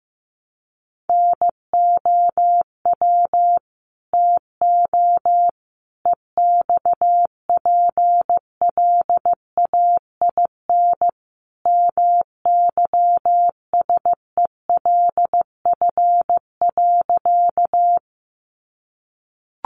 Morse code, heard as a keyed tone, about 14 wpm, 700 Hz. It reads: NOW TO EXPLAIN MYSELF.
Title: Sentences from Top 1300 Words 15wpm